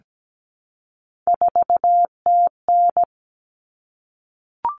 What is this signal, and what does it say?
Morse code, heard as a keyed tone, about 17 wpm, 700 Hz. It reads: 4TN
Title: Common Cut Numbers - 3-Digit 17wpm